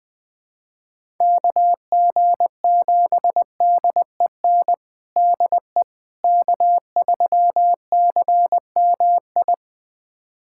Morse code, heard as a keyed tone, about 21 wpm, 700 Hz. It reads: KG7DEN DE K3CMI
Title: USA Callsign DE Callsign 20wpm